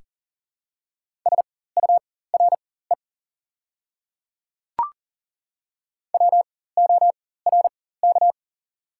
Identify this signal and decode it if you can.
Morse code, heard as a keyed tone, about 40 wpm, spaced as for 19 wpm, 700 Hz. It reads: SURE WORK
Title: 4x Character Spacing - Top 300 Words - 40wpm